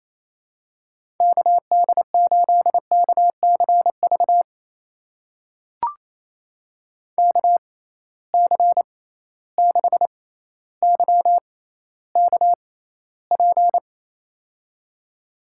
Morse code, heard as a keyed tone, about 27 wpm, spaced as for 14 wpm, 700 Hz. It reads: KD8KCV KC6YKP
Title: USA Call Signs - 6 Characters - 6x Character Spacing - 28wpm